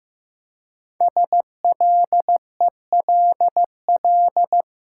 Morse code, heard as a keyed tone, about 15 wpm, 700 Hz. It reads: SLELL
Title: Code Groups - 5 Letters - 15wpm